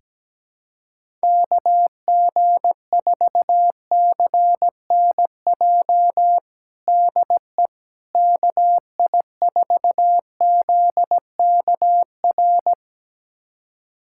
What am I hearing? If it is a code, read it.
Morse code, heard as a keyed tone, about 17 wpm, 700 Hz. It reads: KG4CNJ DE KI4ZKR
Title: USA Callsign DE Callsign - 6 Characters - 17wpm